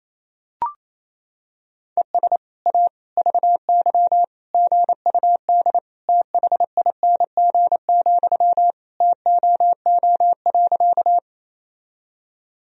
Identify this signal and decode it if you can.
Morse code, heard as a keyed tone, about 28 wpm, 700 Hz. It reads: ES A VY GUD THING, TOO.
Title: Sentences from Top 200 Words and Abbreviations 28wpm